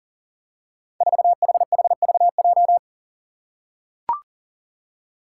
Morse code, heard as a keyed tone, about 37 wpm, 700 Hz. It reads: 4HHVJ
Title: Five Challenging Letter-Number Combinations 40wpm